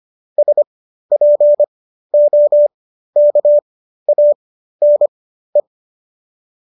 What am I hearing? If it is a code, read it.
Morse code, heard as a keyed tone, about 25 wpm, spaced as for 13 wpm, 590 Hz. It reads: SPOKANE